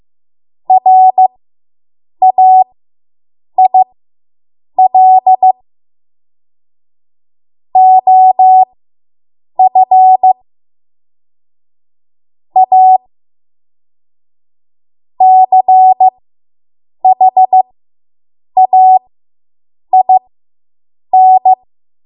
Morse code, heard as a keyed tone, about 15 wpm, spaced as for 7 wpm, 750 Hz. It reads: RAIL OF A CHAIN